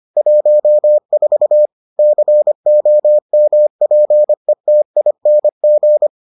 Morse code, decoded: 14 COMPETING

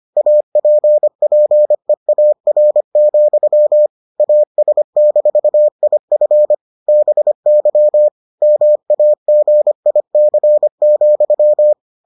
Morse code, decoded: APPEAR, AS-IF BY MAGIC,

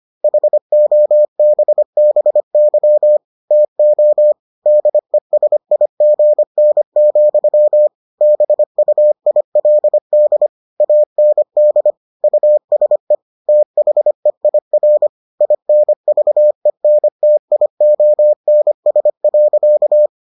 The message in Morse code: HOBBY TO DESIGN, BUILD AND USE THEIR INVENTIONS.

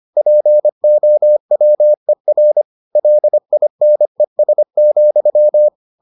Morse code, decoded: POWER LINES,

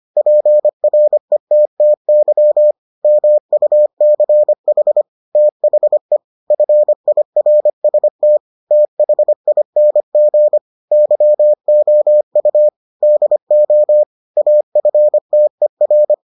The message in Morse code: PRETTY MUCH THE FIRST THING YOU DO AFTER

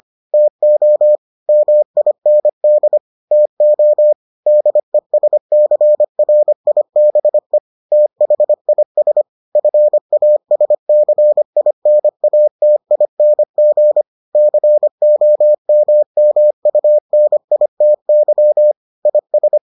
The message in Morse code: TO MIND TO DESCRIBE THIS FASCINATING COMMUNITY IS

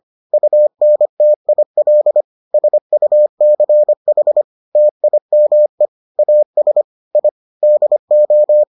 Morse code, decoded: UNTIL SUCH TIME AS I DO

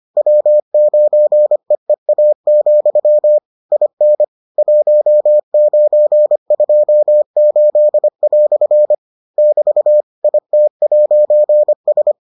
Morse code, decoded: W9EEA, IN 1928" <BT> IT'S

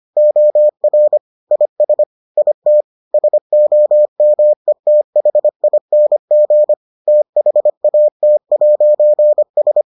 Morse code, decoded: OR IS IT SOMETHING THAT'S

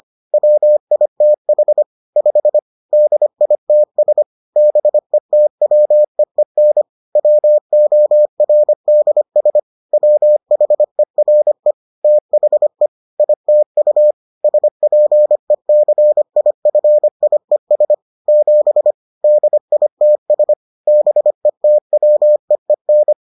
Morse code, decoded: WITH 5 DITS BETWEEN WORDS WHERE THE ITU SPECIFIES 7 DITS BETWEEN

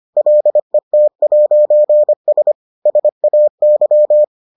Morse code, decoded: LET'S SAY

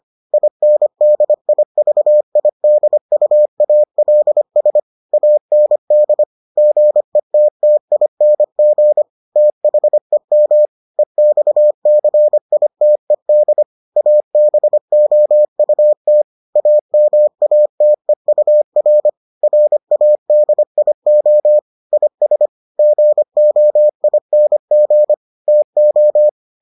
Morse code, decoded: INDIVIDUALS AND GETTING THEM EXCITED ABOUT AMATEUR RADIO IS GOING TO

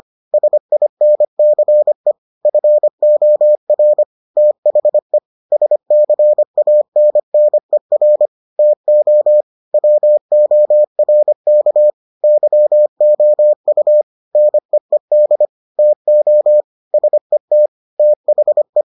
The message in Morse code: SINCE FOR THE SCANNER TO WORK YOU NEED TO SET THE

